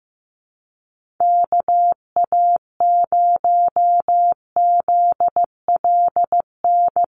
Morse code, heard as a keyed tone, about 15 wpm, 700 Hz. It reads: KA0ZLN
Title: USA Call Signs 15wpm